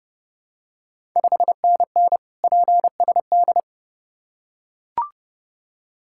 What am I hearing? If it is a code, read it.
Morse code, heard as a keyed tone, about 30 wpm, 690 Hz. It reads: 5NN PSD
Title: Ontario QSO Party Exchange 30wpm